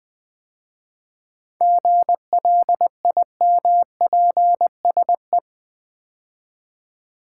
Morse code, heard as a keyed tone, about 20 wpm, 700 Hz. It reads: GLIMPSE
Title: 7 Letter Words - First 3 Letters Given - 20wpm